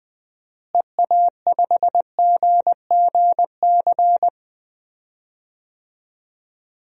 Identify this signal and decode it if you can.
Morse code, heard as a keyed tone, about 20 wpm, 700 Hz. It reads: EA5GGC
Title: DXCC-100 Call Signs - 6 Characters - Repeated 2x - 20wpm